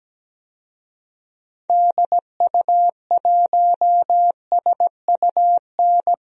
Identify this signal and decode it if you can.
Morse code, heard as a keyed tone, about 17 wpm, 700 Hz. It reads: DU1SUN
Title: DXCC-100 Call Signs - 6 Characters - 17wpm